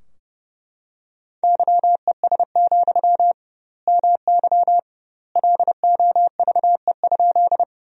Morse code, heard as a keyed tone, about 31 wpm, 700 Hz. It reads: YES, MY LOVE?
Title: 2x Word-Spacing - Sentences from Top 500 Words - 30wpm